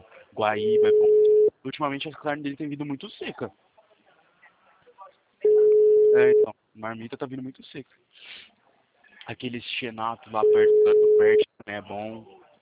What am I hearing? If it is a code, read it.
Ringback tone (Europe, 425 Hz), with speech.
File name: out- -2015- - - .32716